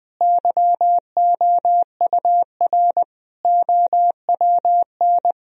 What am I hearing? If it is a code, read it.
Morse code, heard as a keyed tone, about 20 wpm, 700 Hz. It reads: YOUR OWN
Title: Sentences from Top 1000 Words - Review Part 3 - 20wpm